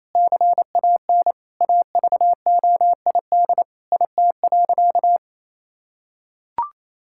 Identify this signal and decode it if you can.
Morse code, heard as a keyed tone, about 28 wpm, 700 Hz. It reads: CAN AVOID IT.
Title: Sentences from Top 1000 Words - Review Part 6 - 28wpm